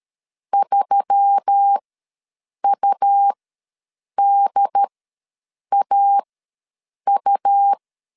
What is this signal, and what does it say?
Morse code, heard as a keyed tone, about 13 wpm, spaced as for 7 wpm, 800 Hz. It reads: 3UDAU
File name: LEZIONI / 5 LBVC / 5 LBVC ws 12wpm 3159